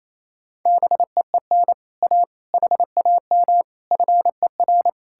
Morse code, heard as a keyed tone, about 37 wpm, 700 Hz. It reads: BEEN A HAM FER